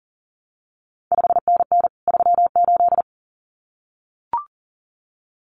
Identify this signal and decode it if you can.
Morse code, heard as a keyed tone, about 42 wpm, 700 Hz. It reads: 5NN 38
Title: CQ WW DX Contest Exchange Encore - 40wpm